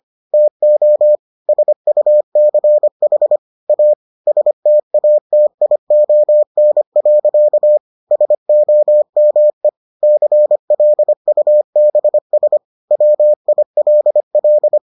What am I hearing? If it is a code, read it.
Morse code, heard as a keyed tone, about 25 wpm, 600 Hz. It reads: TO SUCH A STATION. SOME CLUBS WILL